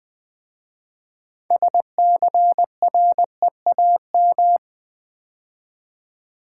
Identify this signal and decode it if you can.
Morse code, heard as a keyed tone, about 20 wpm, 700 Hz. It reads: SCREAM
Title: 6 Letter Words - First 3 Letters Given - 20wpm